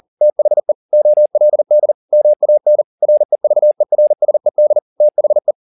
Morse code, heard as a keyed tone, about 40 wpm, 600 Hz. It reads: THE OLD MAN REVERSED THE
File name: the old man and the sea 2 40wpm